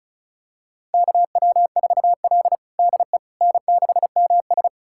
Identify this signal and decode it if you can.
Morse code, heard as a keyed tone, about 34 wpm, 700 Hz. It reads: KW4L DE N6MS
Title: USA Callsign DE Callsign - 4 Characters - 35wpm